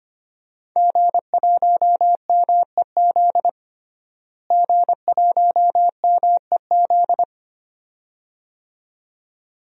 Morse code, heard as a keyed tone, about 25 wpm, 700 Hz. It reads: G1MEZ G1MEZ
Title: DXCC-100 Call Signs - 5 Characters - Repeated 2x - 25wpm